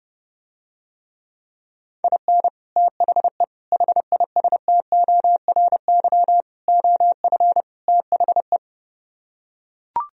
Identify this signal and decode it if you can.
Morse code, heard as a keyed tone, about 30 wpm, 700 Hz. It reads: IN THE HISTORY OF THE